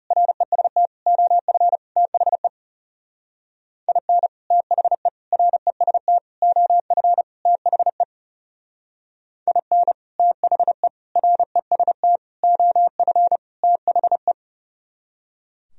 Morse code, heard as a keyed tone, about 38 wpm, 700 Hz. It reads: REST OF THE IN THE REST OF THE IN THE REST OF THE